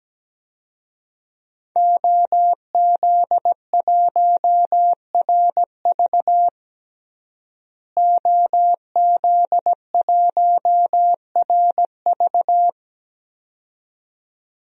Morse code, heard as a keyed tone, about 17 wpm, 700 Hz. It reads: OZ1RV OZ1RV
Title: DXCC-100 Call Signs - 5 Characters - Repeated 2x - 17wpm